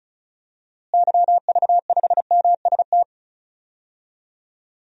Morse code, heard as a keyed tone, about 34 wpm, 700 Hz. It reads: YV5MST
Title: DXCC-100 Call Signs - 6 Characters - Repeated 2x - 35wpm